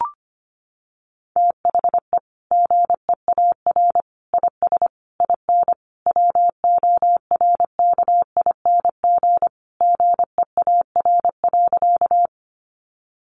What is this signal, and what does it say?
Morse code, heard as a keyed tone, about 25 wpm, 700 Hz. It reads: THE GEAR IS IN WORKING GEAR.
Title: Sentences from Top 1400 Words 25wpm